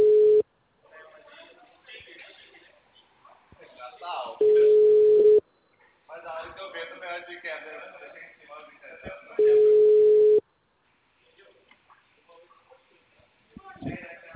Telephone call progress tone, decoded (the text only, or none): ringback tone